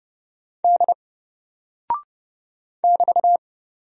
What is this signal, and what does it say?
Morse code, heard as a keyed tone, about 30 wpm, 700 Hz. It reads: D <BT>
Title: Single Character (Letter:Number:Punctuation:Prosign) Alt 30wpm